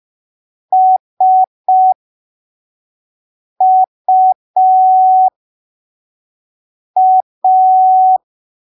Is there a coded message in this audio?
Yes: Morse code from 0.7 to 8.2 s, heard as a keyed tone, about 5 wpm, spaced as for 3 wpm, 750 Hz. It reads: SUA